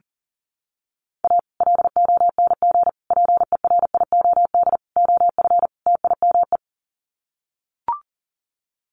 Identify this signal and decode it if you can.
Morse code, heard as a keyed tone, about 40 wpm, 700 Hz. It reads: A LONG PERIOD OF TIME
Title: Sets of 5 Words 40wpm